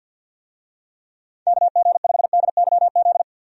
Morse code, heard as a keyed tone, about 45 wpm, 700 Hz. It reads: KG5DYZ
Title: USA Call Signs - 6 Characters - Repeated 2x - 50wpm